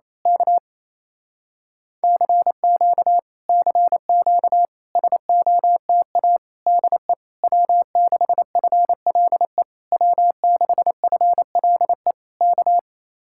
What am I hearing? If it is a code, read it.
Morse code, heard as a keyed tone, about 26 wpm, 700 Hz. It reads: K CQ CQ SOTA DE W6FLE W6FLE K
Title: Speed-Racing Common QSO Phrases Encore - 28wpm